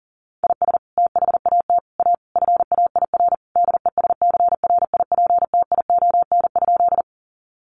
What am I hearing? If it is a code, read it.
Morse code, heard as a keyed tone, about 40 wpm, 700 Hz. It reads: IS THAT A FAIR DESCRIPTION?
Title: Sentences from Top 1300 Words 40wpm